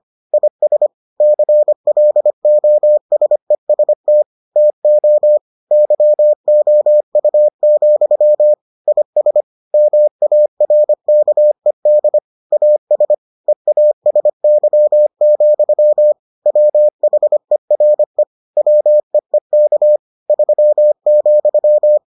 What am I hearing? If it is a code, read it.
Morse code, heard as a keyed tone, about 25 wpm, 600 Hz. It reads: IS CLOSEST TO YOU, IS MARKED AS EASY, WHERE WEEK 3,